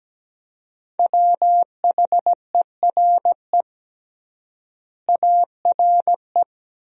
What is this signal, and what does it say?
Morse code, heard as a keyed tone, about 17 wpm, 700 Hz. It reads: WHERE ARE
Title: 3x Word-Spacing - Sentences from Top 200 Words - 17wpm